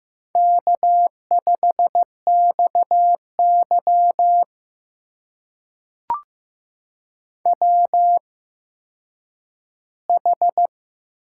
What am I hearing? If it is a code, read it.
Morse code, heard as a keyed tone, about 15 wpm, spaced as for 9 wpm, 700 Hz. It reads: K5XY WH
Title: USA Call Signs - 4 Characters - 8x Character Spacing - 15wpm